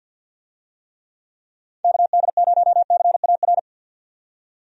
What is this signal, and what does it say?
Morse code, heard as a keyed tone, about 50 wpm, 700 Hz. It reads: KD0CAR
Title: USA Call Signs - 6 Characters - 50wpm